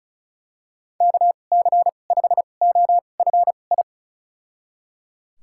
Morse code, heard as a keyed tone, about 34 wpm, spaced as for 25 wpm, 700 Hz. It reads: KC5OFI
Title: USA Call Signs - 6 Characters - 2x Character Spacing - 35wpm